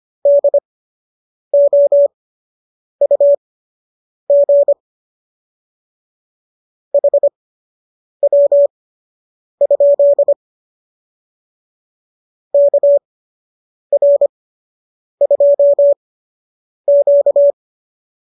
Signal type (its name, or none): Morse code